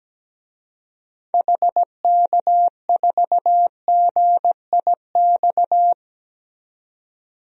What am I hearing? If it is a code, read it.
Morse code, heard as a keyed tone, about 17 wpm, 700 Hz. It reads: HK4GIX
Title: DXCC-100 Call Signs - 6 Characters - 17wpm